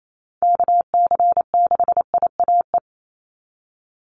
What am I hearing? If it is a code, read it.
Morse code, heard as a keyed tone, about 28 wpm, 700 Hz. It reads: KC6IAE
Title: USA Call Signs 28wpm